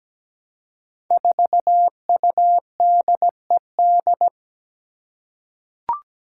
Morse code, heard as a keyed tone, about 17 wpm, 700 Hz. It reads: 4UDED
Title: Five Challenging Letter-Number Combinations 17wpm